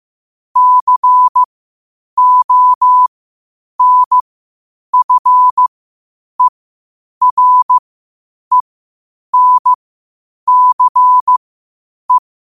Morse code, heard as a keyed tone, about 15 wpm, spaced as for 8 wpm, 1000 Hz. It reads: CONFERENCE
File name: qotdfiles / 8wpm